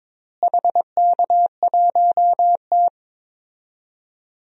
Morse code, heard as a keyed tone, about 22 wpm, 700 Hz. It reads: HK1T
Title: DXCC-100 Call Signs - 4 Characters - Repeated 3x - 22wpm